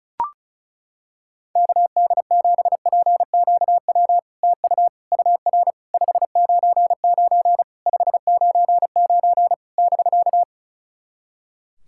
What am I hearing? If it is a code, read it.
Morse code, heard as a keyed tone, about 36 wpm, 700 Hz. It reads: KD7PQW TU UR 599 599 <BK>
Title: SOTA QSO Practice 35wpm